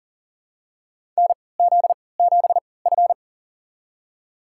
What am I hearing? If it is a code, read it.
Morse code, heard as a keyed tone, about 40 wpm, spaced as for 23 wpm, 700 Hz. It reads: NZ7F
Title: USA Call Signs - 4 Characters - 3x Character Spacing - 40wpm